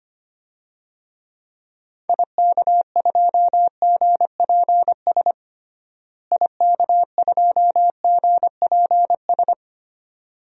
Morse code, heard as a keyed tone, about 25 wpm, 700 Hz. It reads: IK2GPS IK2GPS
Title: DXCC-100 Call Signs - 6 Characters - Repeated 2x - 25wpm